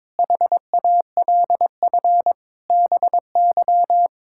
Morse code, heard as a keyed tone, about 22 wpm, 700 Hz. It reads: HALF BY
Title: Sentences from Top 1000 Words - Review Part 5 - 22wpm